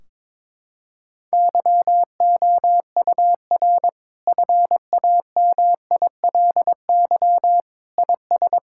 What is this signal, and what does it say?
Morse code, heard as a keyed tone, about 22 wpm, 700 Hz. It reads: YOUR FAMILY IS
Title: Sentences from Top 1600 Words 22wpm